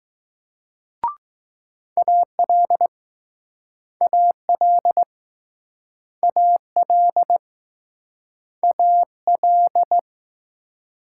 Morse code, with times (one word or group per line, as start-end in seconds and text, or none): 2.0-2.9 AL
4.0-5.0 AL
6.2-7.4 AL
8.6-10.0 AL